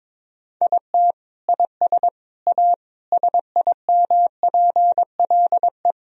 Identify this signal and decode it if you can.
Morse code, heard as a keyed tone, about 21 wpm, 700 Hz. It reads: IT IS A SIMPLE
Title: Sentences from Top 1000 Words - Review Part 2 - 22wpm